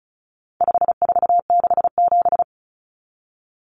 Morse code, heard as a keyed tone, about 34 wpm, 700 Hz. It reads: 5467
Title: Common Cut Numbers - 4-Digit 35wpm